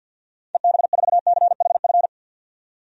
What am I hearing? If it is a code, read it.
Morse code, heard as a keyed tone, about 51 wpm, 700 Hz. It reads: EB4CHF
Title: DXCC-100 Call Signs - 6 Characters - 50wpm